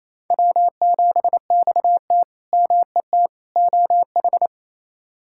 Morse code, heard as a keyed tone, about 28 wpm, 700 Hz. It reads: W7XT MET OH
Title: K1USN SST QSO Practice - 28wpm